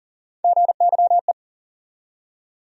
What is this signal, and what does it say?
Morse code, heard as a keyed tone, about 40 wpm, 700 Hz. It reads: GYE
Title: USA Call Sign Postfixes - 3 Letters 40wpm